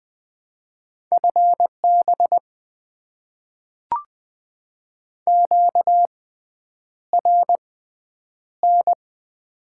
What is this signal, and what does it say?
Morse code, heard as a keyed tone, about 20 wpm, spaced as for 9 wpm, 700 Hz. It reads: FB QRN
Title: Top 90 QSO Elements - 6x Character Spacing - 20wpm